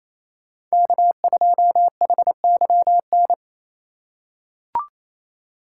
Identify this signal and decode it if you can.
Morse code, heard as a keyed tone, about 28 wpm, 700 Hz. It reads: K2HYN
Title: USA Call Signs - 2x Character Spacing - 28wpm